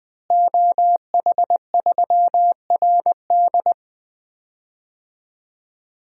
Morse code, heard as a keyed tone, about 20 wpm, 700 Hz. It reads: OH3RD
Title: DXCC-100 Call Signs - 5 Characters - Repeated 3x - 20wpm